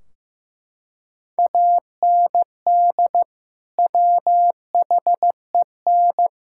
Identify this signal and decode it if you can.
Morse code, heard as a keyed tone, about 15 wpm, 700 Hz. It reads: AND WHEN
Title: Sentences from Top 1600 Words 15wpm